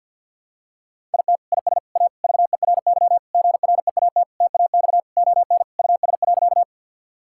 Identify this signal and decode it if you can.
Morse code, heard as a keyed tone, about 50 wpm, 700 Hz. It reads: IT IS A VERY GREAT TAX ON US.
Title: Sentences from Top 1000 Words - Review Part 5 - 50wpm